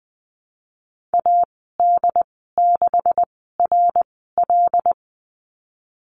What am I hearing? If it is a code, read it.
Morse code, heard as a keyed tone, about 20 wpm, spaced as for 14 wpm, 700 Hz. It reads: AD6RL
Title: 186 Lesson 30g - Letter b - Call Signs 20wpm